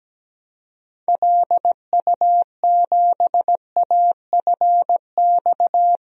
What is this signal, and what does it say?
Morse code, heard as a keyed tone, about 17 wpm, 700 Hz. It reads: LU7AFX
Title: DXCC-100 Call Signs - 6 Characters - 17wpm